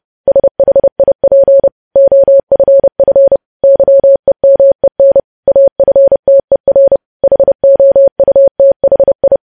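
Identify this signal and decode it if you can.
Morse code, heard as a keyed tone, about 30 wpm, 550 Hz. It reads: SHIP OFF YEMEN AFTER HOUTHI
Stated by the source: News headlines sent as Morse practice at 30.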